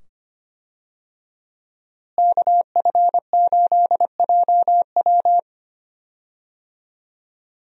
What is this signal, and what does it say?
Morse code, heard as a keyed tone, about 25 wpm, 700 Hz. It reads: KF8JW
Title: USA Call Signs - 5 Characters - Repeated 2x - 25wpm